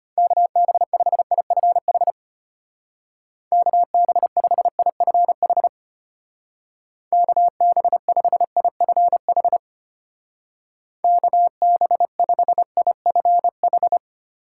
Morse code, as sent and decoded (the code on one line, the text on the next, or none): -.- -... ..... .. ..-. .... / -.- -... ..... .. ..-. .... / -.- -... ..... .. ..-. .... / -.- -... ..... .. ..-. ....
KB5IFH KB5IFH KB5IFH KB5IFH